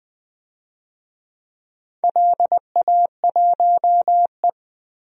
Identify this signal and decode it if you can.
Morse code, heard as a keyed tone, about 21 wpm, 700 Hz. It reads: LA1E
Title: DXCC-100 Call Signs - 4 Characters - 20wpm